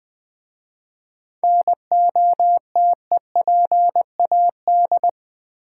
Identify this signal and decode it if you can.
Morse code, heard as a keyed tone, about 20 wpm, 700 Hz. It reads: NOTEPAD
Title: Easy Words with Hints - 20wpm